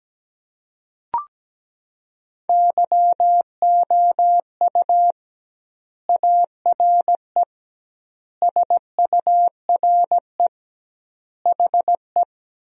Morse code, heard as a keyed tone, about 17 wpm, 700 Hz. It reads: YOU ARE SURE HE
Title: 2x Word-Spacing - Sentences from Top 300 Words - 17wpm